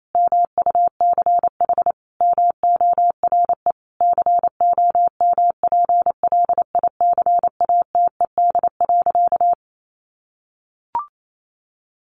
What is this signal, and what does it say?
Morse code, heard as a keyed tone, about 28 wpm, 700 Hz. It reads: MUCH MORE COMPLICATED.
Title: Sentences from Top 1200 Words 28wpm